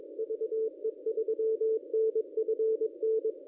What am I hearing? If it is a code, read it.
Morse code, heard as a keyed tone, about 22 wpm, 440 Hz. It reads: VE3NFN
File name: contests / ss-cw-k5tr-op-n5rz -1000